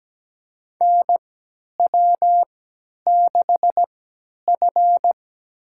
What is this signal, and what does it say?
Morse code, heard as a keyed tone, about 17 wpm, spaced as for 10 wpm, 700 Hz. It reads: NW6F
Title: USA Call Signs - 4 Characters - 3x Character Spacing - 17wpm